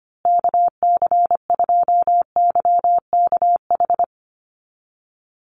Morse code, heard as a keyed tone, about 25 wpm, 700 Hz. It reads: KC2YKH
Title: USA Call Signs Repeated 2x 25wpm